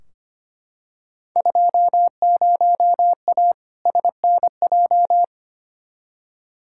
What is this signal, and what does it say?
Morse code, heard as a keyed tone, about 25 wpm, 700 Hz. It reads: 20A SNJ
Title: Field Day Exchanges Encore - 25wpm